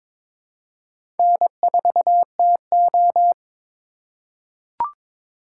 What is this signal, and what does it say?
Morse code, heard as a keyed tone, about 22 wpm, 700 Hz. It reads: N4TO
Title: USA Call Signs - 4x Character Spacing - 22wpm